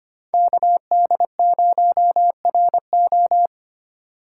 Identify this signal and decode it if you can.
Morse code, heard as a keyed tone, about 25 wpm, 700 Hz. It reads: KD0RO
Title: USA Call Signs - 4x Character Spacing - 25wpm